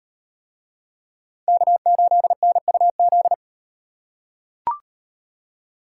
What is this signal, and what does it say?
Morse code, heard as a keyed tone, about 36 wpm, 700 Hz. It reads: K8NUZ